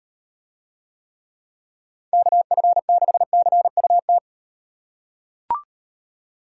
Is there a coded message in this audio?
Yes: Morse code from 2.1 to 4.2 s, heard as a keyed tone, about 38 wpm, 700 Hz. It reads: KF6CUT